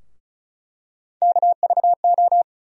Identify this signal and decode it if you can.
Morse code, heard as a keyed tone, about 36 wpm, 700 Hz. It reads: KVO
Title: USA Call Sign Postfixes - 3 Letters 35wpm